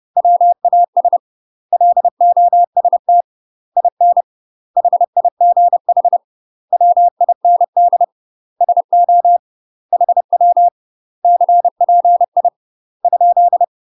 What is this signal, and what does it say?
Morse code, heard as a keyed tone, about 30 wpm, 700 Hz. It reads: WAS LOST IN HIGH WIND SO HW CPI ?
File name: UK MASTER QSOs TRAINING / 30WPM / UK MASTER QSOs 0030